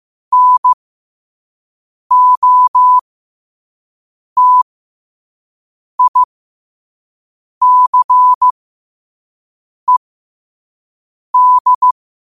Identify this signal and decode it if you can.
Morse code, heard as a keyed tone, about 15 wpm, spaced as for 5 wpm, 1000 Hz. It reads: NOTICED
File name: qotdfiles / 5wpm